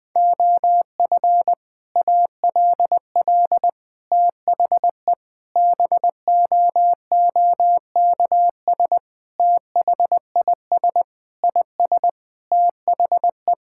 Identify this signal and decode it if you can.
Morse code, heard as a keyed tone, about 20 wpm, 700 Hz. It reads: OF ALL THE BOOKS THIS IS THE
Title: Sentences with 3-Word Phrases - Spoken Beginning - 20wpm